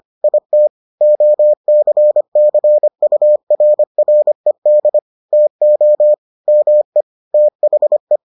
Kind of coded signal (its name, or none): Morse code